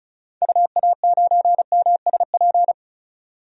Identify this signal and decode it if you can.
Morse code, heard as a keyed tone, about 36 wpm, 700 Hz. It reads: UA9MSP
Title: DXCC-100 Call Signs - 6 Characters - 35wpm